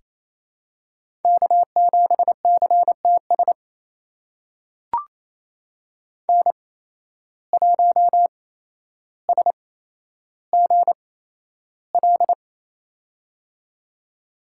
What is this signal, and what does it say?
Morse code, heard as a keyed tone, about 28 wpm, spaced as for 12 wpm, 700 Hz. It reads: K7CTS N1SGL
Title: USA Call Signs - 5 Characters - 8x Character Spacing - 28wpm